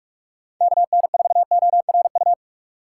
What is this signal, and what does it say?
Morse code, heard as a keyed tone, about 46 wpm, 700 Hz. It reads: KN4ORU
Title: USA Call Signs - 6x Character Spacing - 45wpm